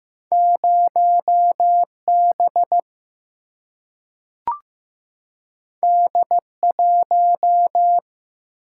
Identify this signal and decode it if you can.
Morse code, heard as a keyed tone, about 15 wpm, 700 Hz. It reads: 0 B D 1
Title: Single-Letter Single-Number 15wpm